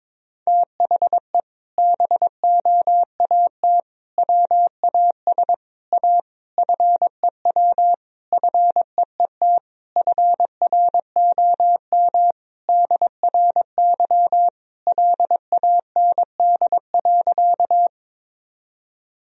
Morse code, heard as a keyed tone, about 22 wpm, 700 Hz. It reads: THE BOAT WAS A FEW FEET FROM DRY LAND.
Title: Sentences from Top 1000 Words - Review Part 3 - 22wpm